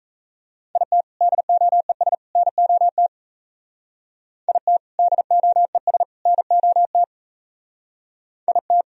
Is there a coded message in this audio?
Yes: Morse code from 0.8 to 8.8 s, heard as a keyed tone, about 36 wpm, 700 Hz. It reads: IT DOES NOT IT DOES NOT IT